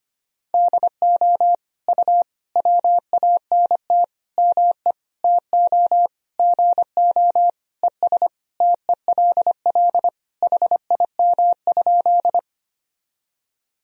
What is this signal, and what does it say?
Morse code, heard as a keyed tone, about 25 wpm, 700 Hz. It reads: DO U WANT ME TO GO ES TELL HIM?
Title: Sentences from Top 200 Words and Abbreviations 25wpm